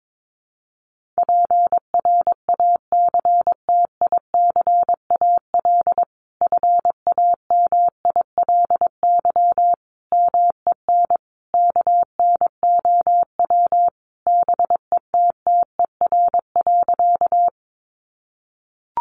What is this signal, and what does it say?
Morse code, heard as a keyed tone, about 22 wpm, 700 Hz. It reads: PRACTICAL FAMILY MEN KNOW BETTER.
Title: Sentences from Top 1400 Words 22wpm